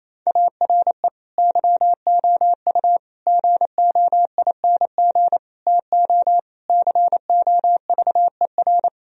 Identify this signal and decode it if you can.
Morse code, heard as a keyed tone, about 28 wpm, 700 Hz. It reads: ARE YOU GOING TO COVER
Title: Sentences from Top 1000 Words - Review Part 1 - 28wpm